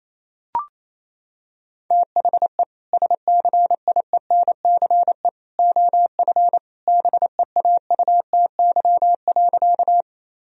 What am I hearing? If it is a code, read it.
Morse code, heard as a keyed tone, about 28 wpm, 700 Hz. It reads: THE SCIENCE OF BEAUTY.
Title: Sentences from Top 1000 Words - Review Part 2 - 28wpm